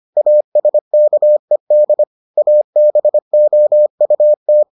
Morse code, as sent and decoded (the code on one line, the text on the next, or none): .- ... -.- . -.. / .- -... --- ..- -
ASKED ABOUT